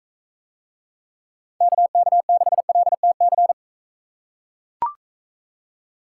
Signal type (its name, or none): Morse code